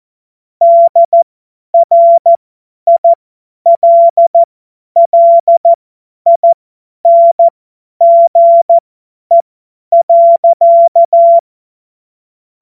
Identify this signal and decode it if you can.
Morse code, heard as a keyed tone, about 14 wpm, spaced as for 10 wpm, 680 Hz. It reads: DRILLINGE.
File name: Morsekurs / Witze / Witz 05 März 2024 50 F70